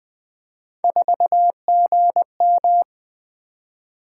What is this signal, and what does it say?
Morse code, heard as a keyed tone, about 21 wpm, 700 Hz. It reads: 4GM
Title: Three Letter-Number Combinations - 20wpm